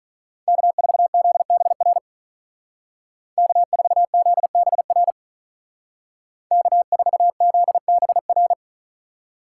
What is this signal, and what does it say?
Morse code, heard as a keyed tone, about 42 wpm, 700 Hz. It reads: K4ZBR K4ZBR K4ZBR
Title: Speed Racing - USA 1x3 Call Signs 35wpm